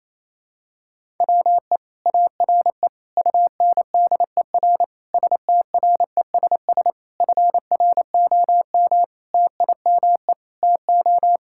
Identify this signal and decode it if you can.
Morse code, heard as a keyed tone, about 28 wpm, 700 Hz. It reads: WE ARE UNDER STRESS FROM TIME TO
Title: Sentences from Top 1000 Words - Review Part 6 - 28wpm